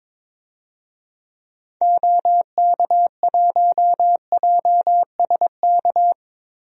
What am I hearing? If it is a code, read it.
Morse code, heard as a keyed tone, about 22 wpm, 700 Hz. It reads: OK1JSK
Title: DXCC-100 Call Signs - 6 Characters - Repeated 2x - 22wpm